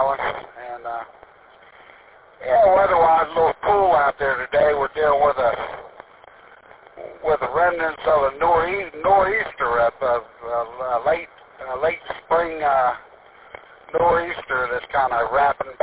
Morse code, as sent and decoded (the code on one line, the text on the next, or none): none